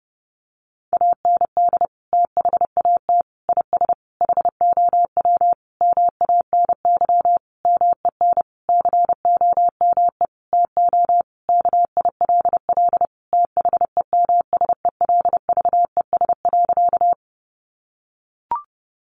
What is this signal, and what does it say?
Morse code, heard as a keyed tone, about 28 wpm, 700 Hz. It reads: AND THAT IS HOW MANY MEN COME TO KILL THEMSELVES.
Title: Sentences from Top 1400 Words 30wpm